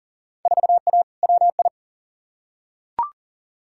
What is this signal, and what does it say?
Morse code, heard as a keyed tone, about 40 wpm, 700 Hz. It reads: 4A WI
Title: Field Day Exchanges - 40wpm